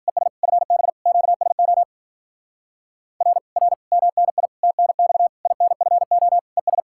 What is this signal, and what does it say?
Morse code, heard as a keyed tone, about 51 wpm, 700 Hz. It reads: ES FB QSO R R MNI TNX INFO ES